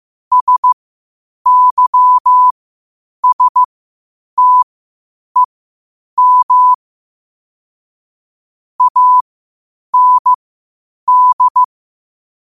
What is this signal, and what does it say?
Morse code, heard as a keyed tone, about 15 wpm, spaced as for 8 wpm, 1000 Hz. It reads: SYSTEM AND
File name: qotdfiles / 8wpm